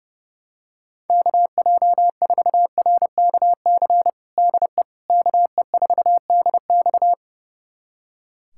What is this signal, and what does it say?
Morse code, heard as a keyed tone, about 30 wpm, 700 Hz. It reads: KJ4RKC DE KE4DX
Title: USA Callsign DE Callsign 30wpm